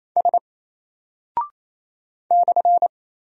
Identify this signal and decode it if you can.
Morse code, heard as a keyed tone, about 27 wpm, 700 Hz. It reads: S /
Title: Single Character (Letter:Number:Punctuation:Prosign) 28wpm